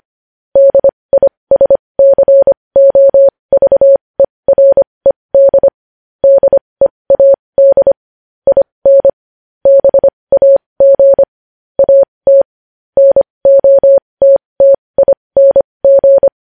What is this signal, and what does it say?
Morse code, heard as a keyed tone, about 25 wpm, spaced as for 20 wpm, 550 Hz. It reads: DISCOVERED DEAD IN BAG AT NOTTING
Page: news-headlines-in-morse-code-at-20-wpm